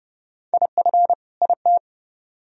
Morse code, heard as a keyed tone, about 29 wpm, 700 Hz. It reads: IF IT